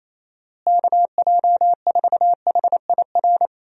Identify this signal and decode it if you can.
Morse code, heard as a keyed tone, about 28 wpm, 700 Hz. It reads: KJ4HIR
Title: USA Call Signs - 6 Characters - Repeated 3x - 28wpm